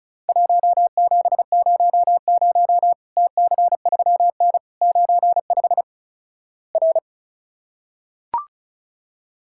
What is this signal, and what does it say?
Morse code, heard as a keyed tone, about 36 wpm, 700 Hz. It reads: 1700 TC3N 95 R
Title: Worked All Europe DX Contest - QTC Exchange - 35wpm